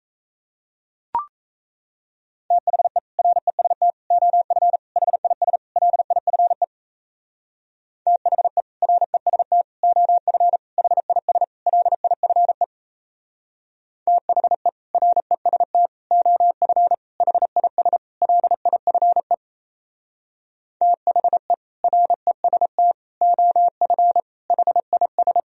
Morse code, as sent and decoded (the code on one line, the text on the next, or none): - .... / .-. ... - / --- ..-. / .... .. ... / .-.. .. ..-. / - .... / .-. ... - / --- ..-. / .... .. ... / .-.. .. ..-. / - .... / .-. ... - / --- ..-. / .... .. ... / .-.. .. ..-. / - .... / .-. ... - / --- ..-. / .... .. ...
THE REST OF HIS LIFE THE REST OF HIS LIFE THE REST OF HIS LIFE THE REST OF HIS